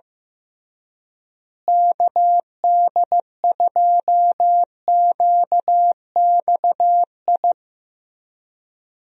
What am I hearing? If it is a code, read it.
Morse code, heard as a keyed tone, about 14 wpm, 700 Hz. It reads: KD2QXI
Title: USA Call Signs - 6 Characters - Repeated 3x - 15wpm